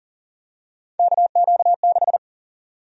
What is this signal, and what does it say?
Morse code, heard as a keyed tone, about 40 wpm, 700 Hz. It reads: KQ6
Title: USA Call Sign Prefixes - 40wpm